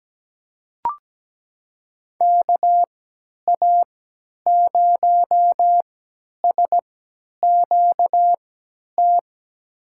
Morse code, heard as a keyed tone, about 17 wpm, spaced as for 10 wpm, 700 Hz. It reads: KA0SQT
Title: USA Call Signs - 6 Characters - 3x Character Spacing - 17wpm